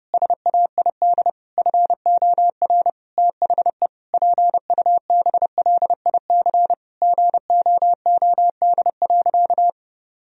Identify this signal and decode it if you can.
Morse code, heard as a keyed tone, about 30 wpm, 700 Hz. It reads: SAID FOR THE PUBLIC GOOD.
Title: Sentences from Top 1000 Words - Review Part 3 - 30wpm